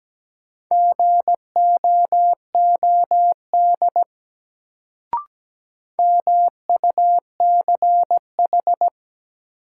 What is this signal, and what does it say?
Morse code, heard as a keyed tone, about 17 wpm, 700 Hz. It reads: GOOD MUCH
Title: Sentences from Top 200 Words - New Word Drill - 17wpm